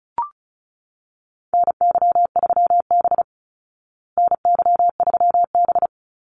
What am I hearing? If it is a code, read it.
Morse code, heard as a keyed tone, about 34 wpm, 700 Hz. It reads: NY3B NY3B
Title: Call Signs Repeated 3x - 35wpm